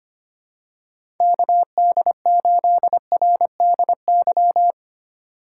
Morse code, heard as a keyed tone, about 25 wpm, 700 Hz. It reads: KD8RDY
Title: USA Call Signs - 6 Characters - 4x Character Spacing - 25wpm